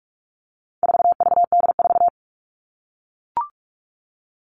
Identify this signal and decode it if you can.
Morse code, heard as a keyed tone, about 44 wpm, 700 Hz. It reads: 4VD4
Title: All Cut Numbers - 4-Digit 45wpm